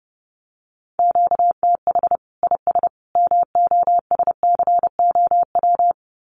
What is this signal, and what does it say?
Morse code, heard as a keyed tone, about 30 wpm, 700 Hz. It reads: QTH IS MOSCOW
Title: Common QSO Phrases 30wpm